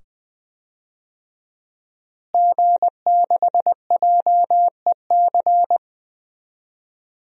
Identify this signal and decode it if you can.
Morse code, heard as a keyed tone, about 20 wpm, 700 Hz. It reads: G6JEC
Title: DXCC-100 Call Signs - 5 Characters - Repeated 2x - 20wpm